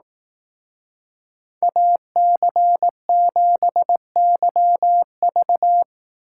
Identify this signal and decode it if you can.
Morse code, heard as a keyed tone, about 18 wpm, 700 Hz. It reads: AC7YV
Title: Speed Racing - USA 2x2 Call Signs 15wpm